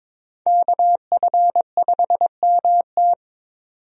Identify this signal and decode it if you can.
Morse code, heard as a keyed tone, about 22 wpm, 700 Hz. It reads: KF5MT
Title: USA Call Signs - 5 Characters - 22wpm